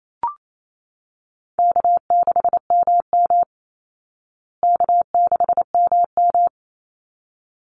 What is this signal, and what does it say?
Morse code, heard as a keyed tone, about 28 wpm, 700 Hz. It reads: K6MM K6MM
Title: Easy Call Signs Repeated 2x - 28wpm